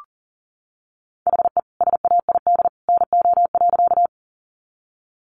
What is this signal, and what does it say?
Morse code, heard as a keyed tone, about 40 wpm, 700 Hz. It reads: HE SAID NO.
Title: Sentences from Top 100 Words 40wpm